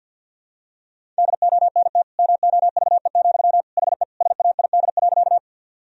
Morse code, heard as a keyed tone, about 46 wpm, 700 Hz. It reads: DONT MOVE, HE SAID.
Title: Sentences from Top 1000 Words - Review Part 1 - 50wpm